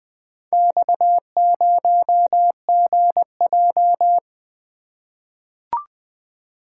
Morse code, heard as a keyed tone, about 20 wpm, 700 Hz. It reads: X0GJ
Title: Four Letter-Number Combinations - 20wpm